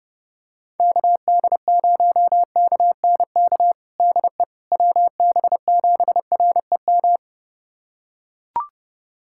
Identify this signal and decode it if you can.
Morse code, heard as a keyed tone, about 30 wpm, 700 Hz. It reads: KD0KNK DE WB7REM
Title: USA Callsign DE Callsign - 6 Characters - 30wpm